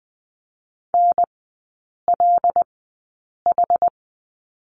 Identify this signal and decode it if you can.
Morse code, heard as a keyed tone, about 20 wpm, spaced as for 8 wpm, 690 Hz. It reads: NLH
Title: 205 Lesson 33e - Letter j - Three Characters 20wpm